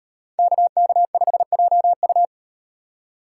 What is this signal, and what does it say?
Morse code, heard as a keyed tone, about 38 wpm, 700 Hz. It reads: KK5JU